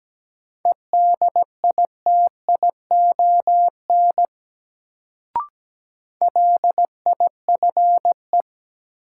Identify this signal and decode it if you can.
Morse code, heard as a keyed tone, about 17 wpm, 700 Hz. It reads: EDITION LIFE
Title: Sets of 5 Words - New Word Drill - 17wpm